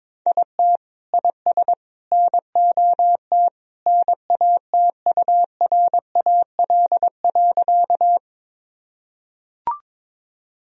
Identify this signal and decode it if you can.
Morse code, heard as a keyed tone, about 22 wpm, 700 Hz. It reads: IT IS NOT NATURAL.
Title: Sentences from Top 1000 Words - Review Part 4 - 22wpm